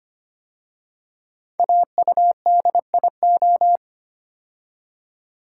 Morse code, heard as a keyed tone, about 25 wpm, 700 Hz. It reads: AUDIO
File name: 5 Letter Words - First 3 Letters Given - 25wpm